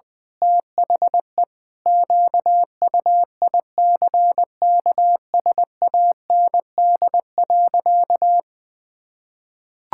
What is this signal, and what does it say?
Morse code, heard as a keyed tone, about 20 wpm, 700 Hz. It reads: THE QUICKSAND.
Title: Pangram Challenge v2 - 20wpm